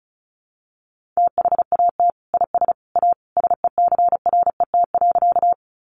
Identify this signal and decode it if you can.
Morse code, heard as a keyed tone, about 34 wpm, 700 Hz. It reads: THAT IS A SECRET.
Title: Sentences from Top 1200 Words 35wpm